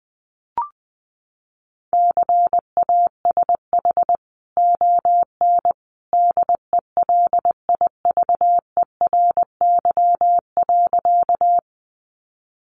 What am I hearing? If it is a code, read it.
Morse code, heard as a keyed tone, about 20 wpm, 700 Hz. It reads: CASH ON DELIVERY.
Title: Sentences from Top 1500 Words 20wpm